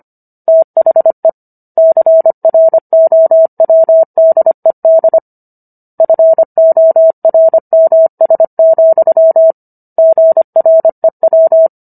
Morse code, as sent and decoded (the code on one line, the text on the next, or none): - .... / -.-. .-. --- .-- -.. -.. / ..-. --- .-. -- ... --..-- / --. .-. .--
THE CROWDED FORMS, GREW